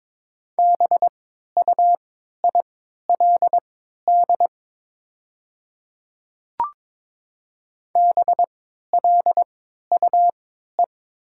Morse code, heard as a keyed tone, about 22 wpm, spaced as for 12 wpm, 700 Hz. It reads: BUILD BLUE
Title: 3x Character Spacing - Top 500 Words - 22wpm